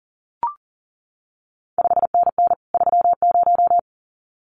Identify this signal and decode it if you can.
Morse code, heard as a keyed tone, about 38 wpm, 700 Hz. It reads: 5NN 30
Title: CQ WW DX Contest Exchange Encore - 40wpm